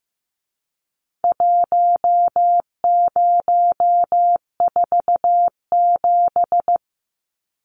Morse code, heard as a keyed tone, about 15 wpm, 700 Hz. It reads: 1047
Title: Numbers - Contest Style - 4-Digit 15wpm